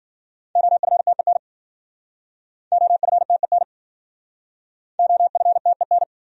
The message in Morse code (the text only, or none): OFTEN OFTEN OFTEN